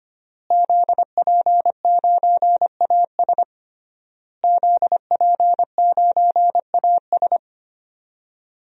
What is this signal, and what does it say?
Morse code, heard as a keyed tone, about 25 wpm, 700 Hz. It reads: ZP9AS ZP9AS
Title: DXCC-100 Call Signs - 5 Characters - Repeated 3x - 25wpm